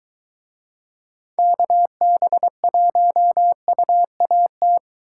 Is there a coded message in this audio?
Yes: Morse code from 1.4 to 4.8 s, heard as a keyed tone, about 23 wpm, 700 Hz. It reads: KB1UAT